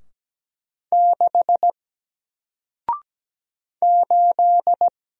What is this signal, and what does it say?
Morse code, heard as a keyed tone, about 17 wpm, 700 Hz. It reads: 6 8
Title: Single Letter-Number Alt 17wpm